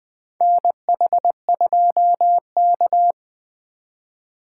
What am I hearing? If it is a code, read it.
Morse code, heard as a keyed tone, about 20 wpm, 700 Hz. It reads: NH2K
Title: USA Call Signs - 4 Characters - 4x Character Spacing - 20wpm